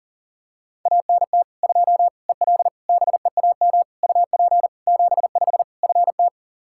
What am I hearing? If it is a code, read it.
Morse code, heard as a keyed tone, about 40 wpm, 700 Hz. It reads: ANT 2 EL BEAM UP 75 FT
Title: K1USN SST QSO Practice 40wpm